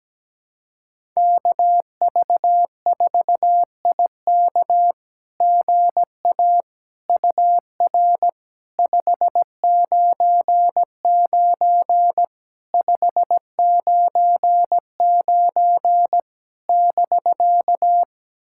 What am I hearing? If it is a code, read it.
Morse code, heard as a keyed tone, about 17 wpm, 700 Hz. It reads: KV4IK GA UR 599 599 <BK>
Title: SOTA QSO Practice 17wpm